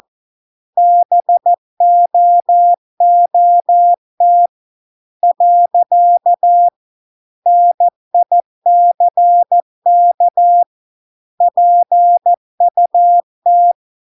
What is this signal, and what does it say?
Morse code, heard as a keyed tone, about 14 wpm, 700 Hz. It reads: BOOT . NICK PUT